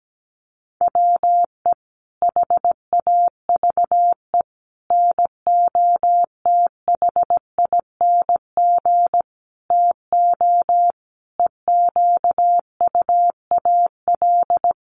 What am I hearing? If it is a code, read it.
Morse code, heard as a keyed tone, about 17 wpm, 700 Hz. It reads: WE HAVE NOTHING TO EQUAL
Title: Sentences from Top 1400 Words 17wpm